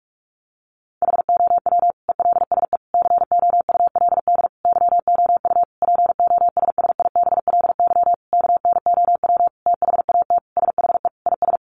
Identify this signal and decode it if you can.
Morse code, heard as a keyed tone, about 44 wpm, 700 Hz. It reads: HOW ELSE COULD YOU POSSIBLY KNOW THAT SHE IS
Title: Sentences from Top 1300 Words 45wpm